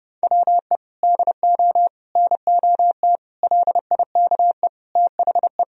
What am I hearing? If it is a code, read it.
Morse code, heard as a keyed tone, about 30 wpm, 700 Hz. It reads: WE DO NOT LIKE THE
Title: Sentences from Top 1000 Words - Review Part 4 - 30wpm